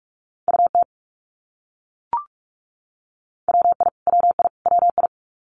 Morse code, heard as a keyed tone, about 44 wpm, 700 Hz. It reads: UT WI WI WI
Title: US States Repeated 3x 45wpm